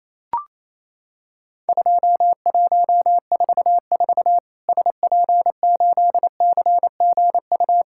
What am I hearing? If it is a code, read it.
Morse code, heard as a keyed tone, about 28 wpm, 700 Hz. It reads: 2144 SP8CGU
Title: Worked All Europe DX Contest - QTC Exchange - 28wpm